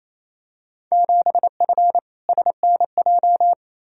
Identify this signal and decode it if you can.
Morse code, heard as a keyed tone, about 27 wpm, 700 Hz. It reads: 7F SNJ
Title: Field Day Exchanges Encore - 28wpm